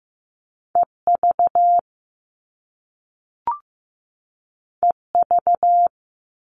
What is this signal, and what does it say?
Morse code, heard as a keyed tone, about 15 wpm, 700 Hz. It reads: EV EV
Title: All Cut Numbers - 2-Digit 15wpm